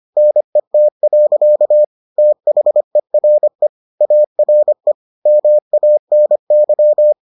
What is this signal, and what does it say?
Morse code, heard as a keyed tone, about 25 wpm, 600 Hz. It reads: NET. THERE ARE MANY